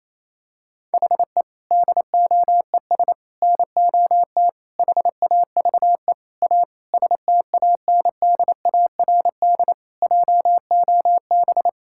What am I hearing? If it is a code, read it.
Morse code, heard as a keyed tone, about 28 wpm, 700 Hz. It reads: HE DOES NOT HAVE A STANDARD JOB
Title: Sentences from Top 1000 Words - Review Part 6 - 28wpm